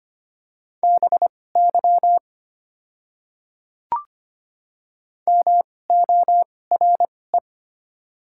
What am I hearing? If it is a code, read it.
Morse code, heard as a keyed tone, about 25 wpm, spaced as for 18 wpm, 700 Hz. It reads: BY MORE
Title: 2x Character Spacing - Top 100 Words - 25wpm